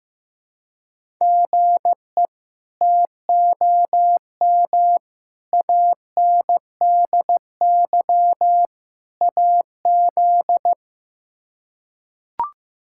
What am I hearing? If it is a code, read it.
Morse code, heard as a keyed tone, about 15 wpm, 700 Hz. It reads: GE TOM ANDY AZ
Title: K1USN SST QSO Practice - 15wpm